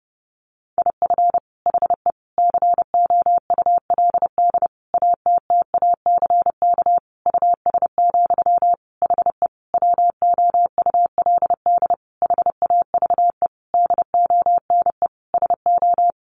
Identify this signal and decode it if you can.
Morse code, heard as a keyed tone, about 28 wpm, 700 Hz. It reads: IF HE COULD ATTACK US, HE WOULD HAVE DONE SO
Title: Sentences from Top 1500 Words 30wpm